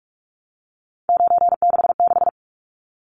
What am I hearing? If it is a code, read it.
Morse code, heard as a keyed tone, about 44 wpm, 700 Hz. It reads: 966
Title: Numbers - 3-Digit 45wpm